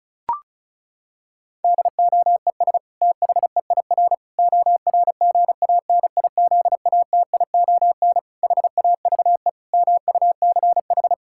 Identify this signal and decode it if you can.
Morse code, heard as a keyed tone, about 34 wpm, 700 Hz. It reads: DOES THEIR ORGANIZATION HAVE MUCH
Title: Sentences from Top 1000 Words - Review Part 5 - 35wpm